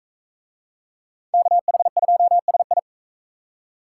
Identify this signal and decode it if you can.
Morse code, heard as a keyed tone, about 41 wpm, 700 Hz. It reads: KH2SI